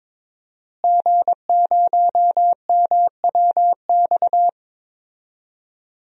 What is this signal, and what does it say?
Morse code, heard as a keyed tone, about 22 wpm, 700 Hz. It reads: G0MWX
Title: DXCC-100 Call Signs - 5 Characters - Repeated 2x - 22wpm